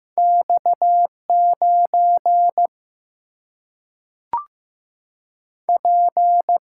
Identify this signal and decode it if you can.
Morse code, heard as a keyed tone, about 15 wpm, 700 Hz. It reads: X9 P
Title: Single-Letter Single-Number 15wpm